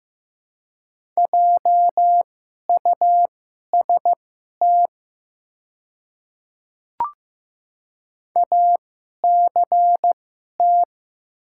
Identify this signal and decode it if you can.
Morse code, heard as a keyed tone, about 15 wpm, spaced as for 11 wpm, 700 Hz. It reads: JUST ACT
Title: 2x Character Spacing - Top 200 Words - 15wpm